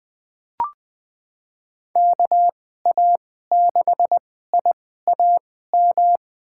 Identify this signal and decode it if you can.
Morse code, heard as a keyed tone, about 20 wpm, spaced as for 14 wpm, 700 Hz. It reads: KA6IAM
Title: USA Call Signs - 6 Characters - 2x Character Spacing - 20wpm